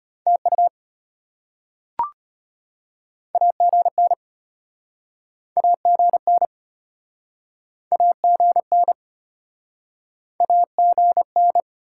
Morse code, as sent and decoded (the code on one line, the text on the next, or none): - ..- / .- --. -. / .- --. -. / .- --. -. / .- --. -.
TU AGN AGN AGN AGN